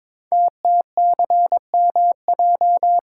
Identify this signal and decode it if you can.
Morse code, heard as a keyed tone, about 22 wpm, 700 Hz. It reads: TTCMJ
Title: Code Groups - 5 Letters - 22wpm